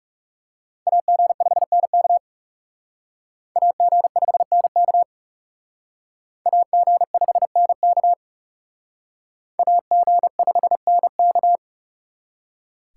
Morse code, heard as a keyed tone, about 38 wpm, 700 Hz. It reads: AG5NK AG5NK AG5NK AG5NK